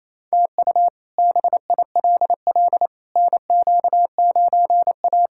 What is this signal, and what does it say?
Morse code, heard as a keyed tone, about 28 wpm, 700 Hz. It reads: TU BILL NQ9A
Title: K1USN SST QSO Practice - 28wpm